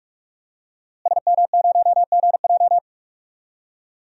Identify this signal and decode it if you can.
Morse code, heard as a keyed tone, about 45 wpm, 700 Hz. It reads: SM0GJ
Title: DXCC-100 Call Signs - 5 Characters - Repeated 3x - 45wpm